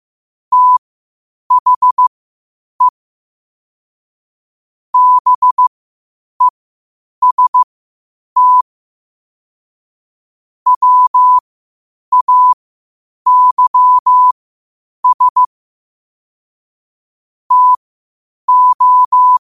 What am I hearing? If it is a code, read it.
Morse code, heard as a keyed tone, about 15 wpm, spaced as for 8 wpm, 1000 Hz. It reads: THE BEST WAYS TO
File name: qotdfiles / 8wpm